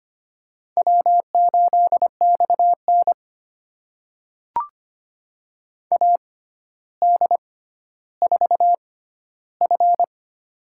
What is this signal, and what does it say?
Morse code, heard as a keyed tone, about 25 wpm, spaced as for 13 wpm, 700 Hz. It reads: W8XN AD4F
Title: USA Call Signs - 4 Characters - 6x Character Spacing - 25wpm